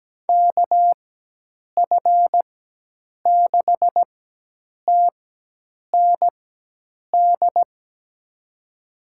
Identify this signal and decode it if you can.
Morse code, heard as a keyed tone, about 17 wpm, spaced as for 8 wpm, 700 Hz. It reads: KF6TND
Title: USA Call Signs - 6 Characters - 4x Character Spacing - 17wpm